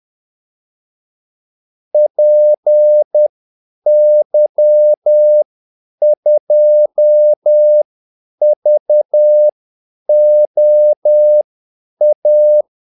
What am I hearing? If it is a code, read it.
Morse code, heard as a keyed tone, about 10 wpm, spaced as for 8 wpm, 600 Hz. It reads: PY2VOA
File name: Callsigns-10-8wpm